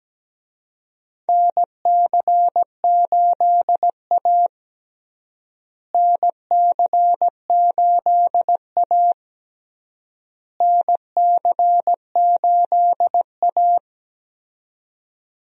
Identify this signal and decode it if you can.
Morse code, heard as a keyed tone, about 17 wpm, 700 Hz. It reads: NC8A NC8A NC8A
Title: USA Call Signs - 4 Characters - Repeated 3x - 17wpm